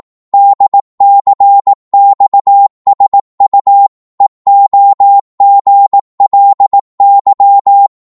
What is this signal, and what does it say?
Morse code, heard as a keyed tone, about 18 wpm, 800 Hz. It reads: DCXSU EOGLY
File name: Morse / ltr-18wpm